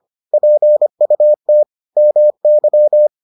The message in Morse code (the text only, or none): PUT MY